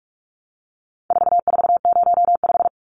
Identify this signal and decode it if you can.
Morse code, heard as a keyed tone, about 44 wpm, 700 Hz. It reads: 4405
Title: Numbers - 4-Digit 45wpm